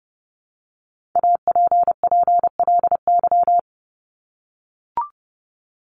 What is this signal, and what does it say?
Morse code, heard as a keyed tone, about 30 wpm, 700 Hz. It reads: APPLY
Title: ED Suffix - Top 100 Words with Base Words 30wpm